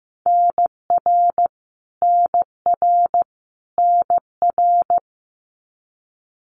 Morse code, heard as a keyed tone, about 15 wpm, 700 Hz. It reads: NR NR NR
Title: Abbreviations Repeated 3x - 15wpm